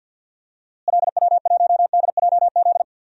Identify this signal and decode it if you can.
Morse code, heard as a keyed tone, about 46 wpm, 700 Hz. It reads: LW1DJZ
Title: DXCC-100 Call Signs - 6 Characters - Repeated 2x - 50wpm